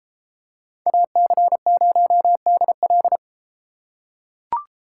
Morse code, heard as a keyed tone, about 30 wpm, 700 Hz. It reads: AC0DL